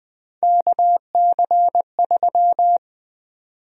Morse code, heard as a keyed tone, about 20 wpm, 700 Hz. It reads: KC3
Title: USA Call Sign Prefixes - Repeated 2x - 20wpm